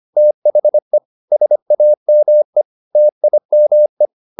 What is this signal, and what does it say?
Morse code, heard as a keyed tone, about 25 wpm, 600 Hz. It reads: THE SAME TIME